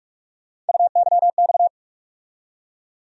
Morse code, heard as a keyed tone, about 46 wpm, 700 Hz. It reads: UYX
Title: USA Call Sign Postfixes - 3 Letters 45wpm